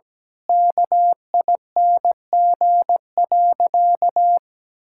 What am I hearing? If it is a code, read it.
Morse code, heard as a keyed tone, about 17 wpm, 700 Hz. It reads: KING.
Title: Sentences from Top 1000 Words - Review Part 6 - 17wpm